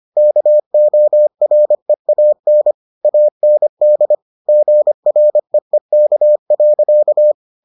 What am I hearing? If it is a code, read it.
Morse code, heard as a keyed tone, about 25 wpm, 600 Hz. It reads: KOREAN AND GREEK.